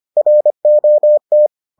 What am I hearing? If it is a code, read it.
Morse code, heard as a keyed tone, about 25 wpm, 600 Hz. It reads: ROT